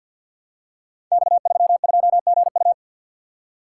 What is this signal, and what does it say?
Morse code, heard as a keyed tone, about 53 wpm, 700 Hz. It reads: X32GU
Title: Code Groups - 5 Letter:Numbers - 50wpm